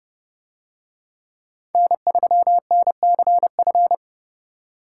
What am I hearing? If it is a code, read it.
Morse code, heard as a keyed tone, about 29 wpm, 700 Hz. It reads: N3NCF